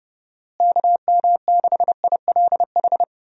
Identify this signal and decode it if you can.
Morse code, heard as a keyed tone, about 31 wpm, 700 Hz. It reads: KM6ILH
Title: USA Call Signs - 6 Characters - Repeated 2x - 30wpm